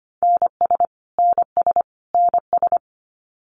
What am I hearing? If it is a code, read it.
Morse code, heard as a keyed tone, about 25 wpm, 700 Hz. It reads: NS NS NS
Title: Canadian Provinces and Territories - Repeated 3x - 25wpm